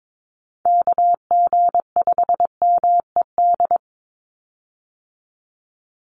Morse code, heard as a keyed tone, about 22 wpm, 700 Hz. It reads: KG5MED
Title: USA Call Signs 22wpm